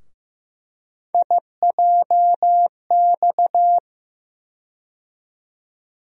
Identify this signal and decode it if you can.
Morse code, heard as a keyed tone, about 15 wpm, 700 Hz. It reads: IJX
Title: USA Call Sign Postfixes - 3 Letters 15wpm